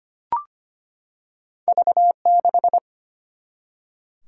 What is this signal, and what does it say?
Morse code, heard as a keyed tone, about 25 wpm, 700 Hz. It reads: V6
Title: Single-Letter Single-Number 25wpm